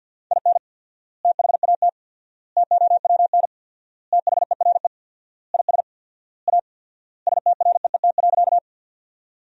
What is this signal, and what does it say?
Morse code, heard as a keyed tone, about 50 wpm, 700 Hz. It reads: IN THAT TOWN THERE IS A STREET.
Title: 4x Word-Spacing - Sentences from Top 500 Words - 50wpm